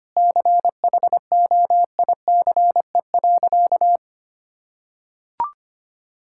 Morse code, heard as a keyed tone, about 25 wpm, 700 Hz. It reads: CHOICE.
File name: practice / 25wpm / Sentences with 3-Word Phrases - 25wpm